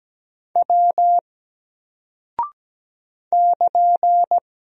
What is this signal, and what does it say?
Morse code, heard as a keyed tone, about 17 wpm, 700 Hz. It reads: W <KN>
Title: Single Character (Letter:Number:Punctuation:Prosign) 17wpm